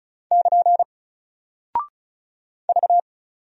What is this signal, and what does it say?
Morse code, heard as a keyed tone, about 36 wpm, 700 Hz. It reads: <KN> V
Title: Single Character (Letter:Number:Punctuation:Prosign) 35wpm